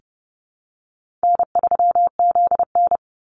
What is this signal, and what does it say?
Morse code, heard as a keyed tone, about 30 wpm, 700 Hz. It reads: N3ZN
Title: Easy Call Signs Repeated 3x - 30wpm